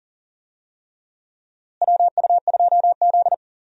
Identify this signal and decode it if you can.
Morse code, heard as a keyed tone, about 40 wpm, 700 Hz. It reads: WU2Z
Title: USA Call Signs - 4 Characters - Repeated 3x - 40wpm